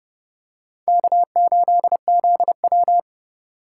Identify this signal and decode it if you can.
Morse code, heard as a keyed tone, about 27 wpm, 700 Hz. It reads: K8ZW